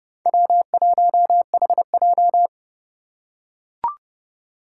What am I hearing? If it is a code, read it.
Morse code, heard as a keyed tone, about 30 wpm, 700 Hz. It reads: W1HJ